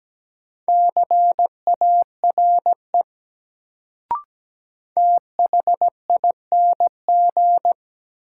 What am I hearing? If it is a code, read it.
Morse code, heard as a keyed tone, about 17 wpm, 700 Hz. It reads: CARE THING
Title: Sets of 3 Words - New Word Drill - 17wpm